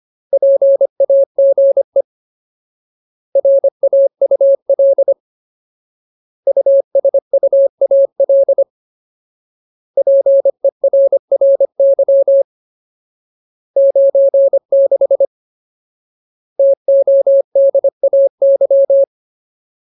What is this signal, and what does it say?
Morse code, heard as a keyed tone, about 25 wpm, 550 Hz. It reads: PAGE RAUL USUAL PERRY 96 TODAY
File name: Ultimate Word List / 25WPM Ultimate